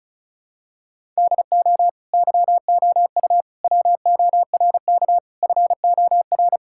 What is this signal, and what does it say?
Morse code, heard as a keyed tone, about 36 wpm, 700 Hz. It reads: DO YOU WORK FOR
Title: Sentences from Top 1000 Words - Review Part 5 - 35wpm